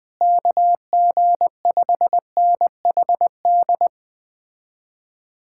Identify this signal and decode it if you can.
Morse code, heard as a keyed tone, about 20 wpm, 700 Hz. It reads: KG5NHD
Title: Speed-Racing Common QSO Phrases Encore - 17wpm